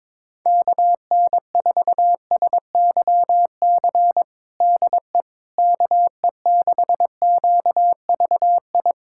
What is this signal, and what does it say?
Morse code, heard as a keyed tone, about 22 wpm, 700 Hz. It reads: KN4SYC DE KE6QVI
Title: USA Callsign DE Callsign - 6 Characters - 22wpm